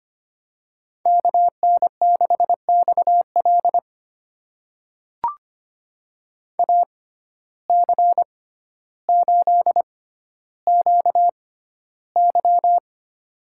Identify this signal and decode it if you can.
Morse code, heard as a keyed tone, about 25 wpm, spaced as for 13 wpm, 700 Hz. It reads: KN6XL AC8QY
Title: USA Call Signs - 5 Characters - 6x Character Spacing - 25wpm